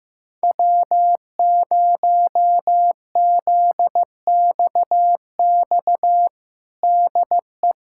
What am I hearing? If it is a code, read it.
Morse code, heard as a keyed tone, about 15 wpm, 700 Hz. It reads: W0ZXX DE
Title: USA Callsign DE Callsign - 5 Characters - 15wpm